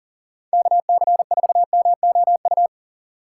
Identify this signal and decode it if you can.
Morse code, heard as a keyed tone, about 39 wpm, 700 Hz. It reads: KC4MOU
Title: USA Call Signs - 6 Characters - 4x Character Spacing - 40wpm